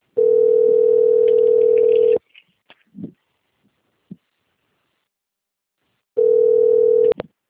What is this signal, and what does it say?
Ringback tone (North America), 440 + 480 Hz.